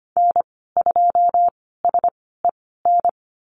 Morse code, heard as a keyed tone, about 25 wpm, spaced as for 16 wpm, 690 Hz. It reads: N2SEN
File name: Lesson 15g - Number 5 - Call Signs 25wpm